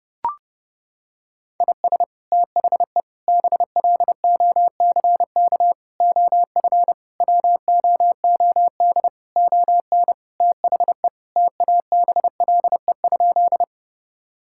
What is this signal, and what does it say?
Morse code, heard as a keyed tone, about 32 wpm, 700 Hz. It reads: IS THE BLOCK OF WOOD ON THE TABLE?
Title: Sentences from Top 1600 Words 30wpm